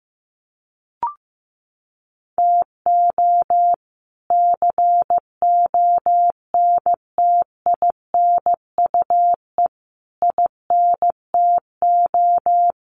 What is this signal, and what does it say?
Morse code, heard as a keyed tone, about 15 wpm, 700 Hz. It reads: TO CONTINUE INTO
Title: Sentences from Top 1500 Words 15wpm